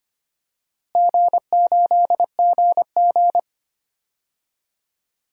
Morse code, heard as a keyed tone, about 25 wpm, 700 Hz. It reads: G8GG
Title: DXCC-100 Call Signs - 4 Characters - Repeated 2x - 25wpm